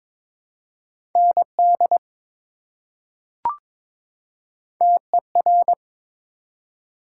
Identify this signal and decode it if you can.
Morse code, heard as a keyed tone, about 22 wpm, 700 Hz. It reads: ND TER
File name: ARRL Sections v2 - 22wpm